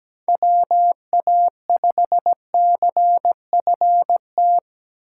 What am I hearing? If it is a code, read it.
Morse code, heard as a keyed tone, about 20 wpm, 700 Hz. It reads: WA5CFT
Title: Speed Racing - USA 2x3 Call Signs 17wpm